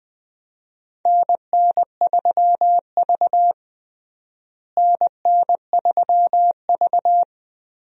Morse code, heard as a keyed tone, about 20 wpm, 700 Hz. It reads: NN3V NN3V
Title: USA Call Signs - 4 Characters - Repeated 2x - 20wpm